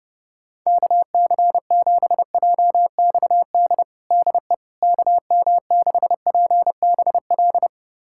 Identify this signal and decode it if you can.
Morse code, heard as a keyed tone, about 32 wpm, 700 Hz. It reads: KC7JXD DE KM6PBL
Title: USA Callsign DE Callsign - 6 Characters - 30wpm